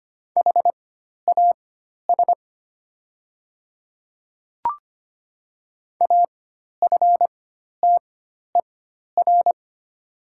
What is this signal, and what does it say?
Morse code, heard as a keyed tone, about 25 wpm, spaced as for 12 wpm, 700 Hz. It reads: HAS AFTER
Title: 4x Character Spacing - Top 300 Words - 25wpm